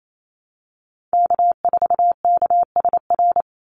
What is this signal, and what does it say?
Morse code, heard as a keyed tone, about 28 wpm, 700 Hz. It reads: K4KSR
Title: Call Signs Repeated 3x - 28wpm